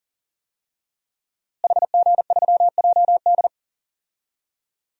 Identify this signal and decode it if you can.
Morse code, heard as a keyed tone, about 40 wpm, 700 Hz. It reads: HG3JD
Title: DXCC-100 Call Signs - 5 Characters - Repeated 3x - 40wpm